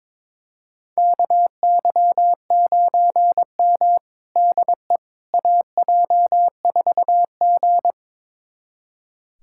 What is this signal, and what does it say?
Morse code, heard as a keyed tone, about 22 wpm, 700 Hz. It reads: KY9M DE AJ4G
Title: USA Callsign DE Callsign - 4 Characters - 22wpm